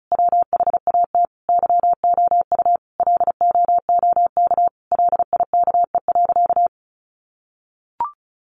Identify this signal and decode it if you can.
Morse code, heard as a keyed tone, about 34 wpm, 700 Hz. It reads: WHAT YOU LOOK LIKE.
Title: Sentences from Top 1300 Words 35wpm